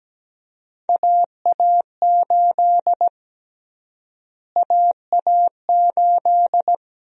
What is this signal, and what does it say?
Morse code, heard as a keyed tone, about 17 wpm, 700 Hz. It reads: AA8 AA8
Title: USA Call Sign Prefixes - Repeated 2x - 17wpm